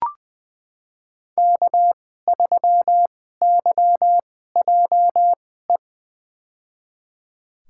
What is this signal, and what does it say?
Morse code, heard as a keyed tone, about 20 wpm, spaced as for 15 wpm, 700 Hz. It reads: K3YJE
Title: USA Call Signs - 5 Characters - 2x Character Spacing - 20wpm